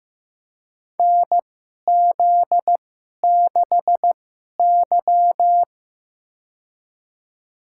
Morse code, heard as a keyed tone, about 15 wpm, spaced as for 11 wpm, 700 Hz. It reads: NZ6Y
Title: USA Call Signs - 4 Characters - 2x Character Spacing - 15wpm